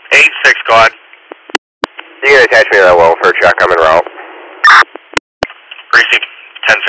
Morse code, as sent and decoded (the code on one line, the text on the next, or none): none